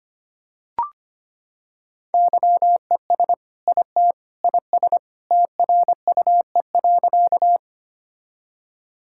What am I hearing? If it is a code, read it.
Morse code, heard as a keyed tone, about 25 wpm, 700 Hz. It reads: YES IT IS TRUE.
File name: practice / 25wpm / Sentences with 2-Word Phrases - 25wpm